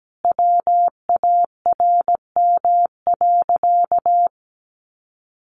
Morse code, heard as a keyed tone, about 17 wpm, 700 Hz. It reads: WARM.
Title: Sentences from Top 1500 Words 17wpm